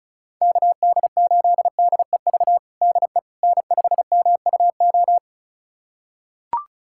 Morse code, heard as a keyed tone, about 36 wpm, 700 Hz. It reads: KD8DEV DE N5MUO
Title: USA Callsign DE Callsign 35wpm